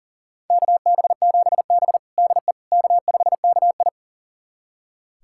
Morse code, heard as a keyed tone, about 37 wpm, 700 Hz. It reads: KB7B DE K5KI
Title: USA Callsign DE Callsign - 4 Characters - 40wpm